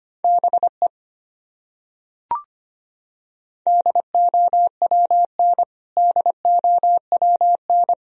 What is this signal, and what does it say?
Morse code, heard as a keyed tone, about 25 wpm, 700 Hz. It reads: BE DOWN DOWN
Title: Top 100 Words Repeated 25wpm